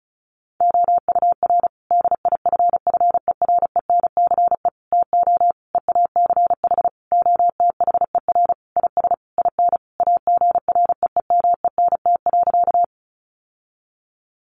Morse code, heard as a keyed tone, about 34 wpm, 700 Hz. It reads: OUR DIFFERENCE TO EACH OTHER IS IN AGREEMENT.
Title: Sentences from Top 1500 Words 35wpm